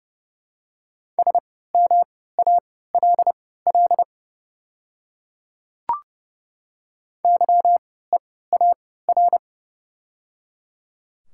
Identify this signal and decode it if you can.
Morse code, heard as a keyed tone, about 30 wpm, spaced as for 17 wpm, 700 Hz. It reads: SMALL YEAR
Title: 3x Character Spacing - Top 300 Words - 30wpm